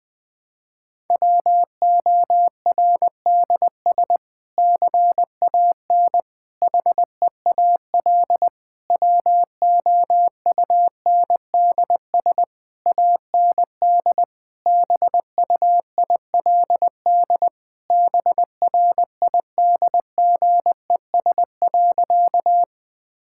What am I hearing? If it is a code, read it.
Morse code, heard as a keyed tone, about 20 wpm, 700 Hz. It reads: WORDS CAN HEAL WOUNDS AND BUILD BRIDGES.
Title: Difficult Sentences with Hints - 20wpm